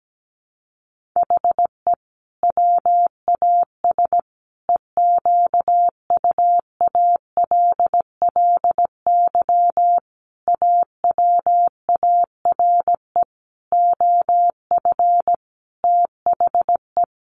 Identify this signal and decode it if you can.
Morse code, heard as a keyed tone, about 17 wpm, 700 Hz. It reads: HE WAS EQUALLY AWARE OF THE
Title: Sentences from Top 1400 Words 17wpm